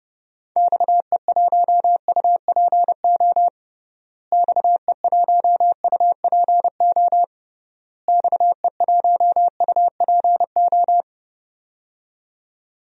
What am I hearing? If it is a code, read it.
Morse code, heard as a keyed tone, about 30 wpm, 700 Hz. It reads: XE1UPO XE1UPO XE1UPO
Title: DXCC-100 Call Signs - 6 Characters - Repeated 3x - 30wpm